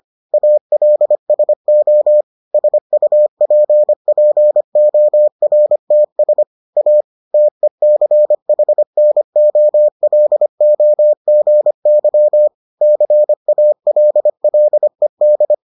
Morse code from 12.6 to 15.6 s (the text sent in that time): CALLED